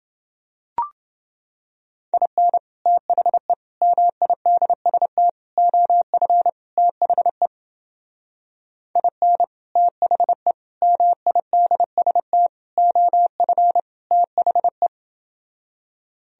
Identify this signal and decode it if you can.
Morse code, heard as a keyed tone, about 29 wpm, 700 Hz. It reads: IN THE MIDST OF THE IN THE MIDST OF THE